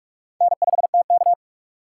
Morse code, heard as a keyed tone, about 47 wpm, 700 Hz. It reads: N5TK
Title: USA Call Signs - 4 Characters - 2x Character Spacing - 45wpm